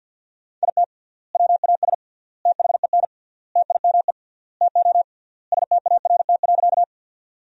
Morse code, heard as a keyed tone, about 49 wpm, 700 Hz. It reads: IT WAS THEN TIME TO START.
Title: 3x Word-Spacing - Sentences from Top 300 Words - 50wpm